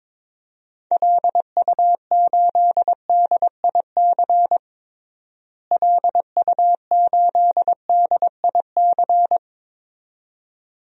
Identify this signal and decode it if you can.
Morse code, heard as a keyed tone, about 22 wpm, 700 Hz. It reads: LU8DIC LU8DIC
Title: DXCC-100 Call Signs - 6 Characters - Repeated 2x - 22wpm